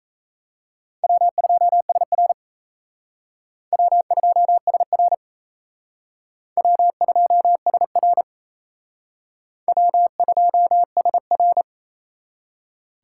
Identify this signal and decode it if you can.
Morse code, heard as a keyed tone, about 35 wpm, 700 Hz. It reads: W2SR W2SR W2SR W2SR